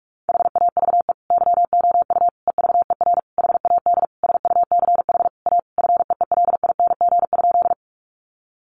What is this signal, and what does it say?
Morse code, heard as a keyed tone, about 44 wpm, 700 Hz. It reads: HAVE YOU EVER HAD SUCH A FEELING?
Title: Sentences from Top 1200 Words 45wpm